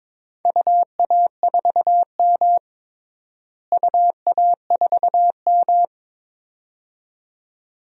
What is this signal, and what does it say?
Morse code, heard as a keyed tone, about 22 wpm, 700 Hz. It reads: UA4M UA4M
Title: DXCC-100 Call Signs - 4 Characters - Repeated 2x - 22wpm